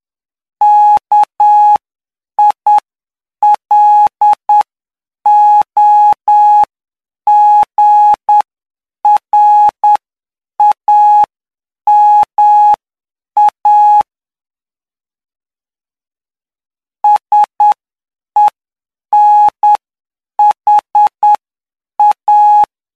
Morse code, heard as a keyed tone, about 9 wpm, spaced as for 7 wpm, 840 Hz. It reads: KILOGRAMA SENHA